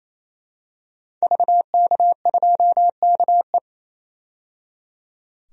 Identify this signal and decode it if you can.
Morse code, heard as a keyed tone, about 28 wpm, 700 Hz. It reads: VK2KE
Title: DXCC-100 Call Signs - 5 Characters - 28wpm